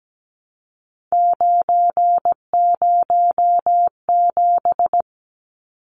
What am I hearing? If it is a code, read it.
Morse code, heard as a keyed tone, about 17 wpm, 700 Hz. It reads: 907
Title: Numbers - 3-Digit 17wpm